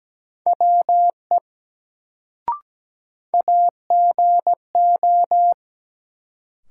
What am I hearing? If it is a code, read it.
Morse code, heard as a keyed tone, about 17 wpm, 700 Hz. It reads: WE AGO
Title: Sets of 2 Words - New Word Drill - 17wpm